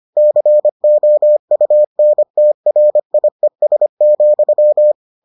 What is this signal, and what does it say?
Morse code, heard as a keyed tone, about 25 wpm, 600 Hz. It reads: COUNTRIES,